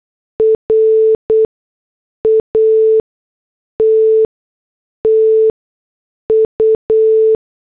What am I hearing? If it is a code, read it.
Morse code, heard as a keyed tone, about 8 wpm, spaced as for 6 wpm, 440 Hz. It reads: RATTU
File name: morse / headlines